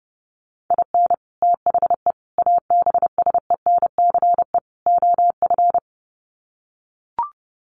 Morse code, heard as a keyed tone, about 30 wpm, 700 Hz. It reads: IN THE ABSENCE OF
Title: Sets of 4 Words 30wpm